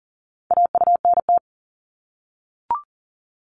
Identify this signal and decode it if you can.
Morse code, heard as a keyed tone, about 41 wpm, 700 Hz. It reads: AUNT
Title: Multi-letter Phonemes 40wpm